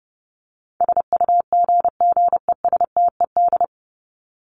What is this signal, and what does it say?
Morse code, heard as a keyed tone, about 30 wpm, 700 Hz. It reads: SUGGESTED
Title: ED Suffix - Top 100 Words with Base Words 30wpm